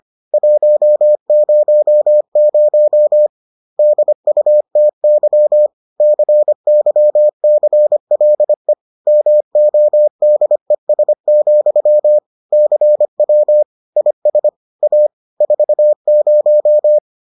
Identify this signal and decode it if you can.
Morse code, heard as a keyed tone, about 25 wpm, 600 Hz. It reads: 100 DUTY CYCLE MODES, CW IS A 40